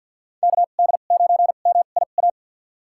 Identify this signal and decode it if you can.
Morse code, heard as a keyed tone, about 46 wpm, spaced as for 35 wpm, 700 Hz. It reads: KD9MIA
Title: USA Call Signs - 2x Character Spacing - 50wpm